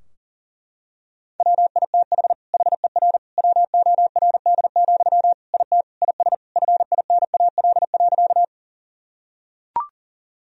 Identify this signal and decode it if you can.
Morse code, heard as a keyed tone, about 40 wpm, 700 Hz. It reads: WITH HER WORD, IT IS FINAL.
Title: Sentences from Top 1000 Words - Review Part 2 - 40wpm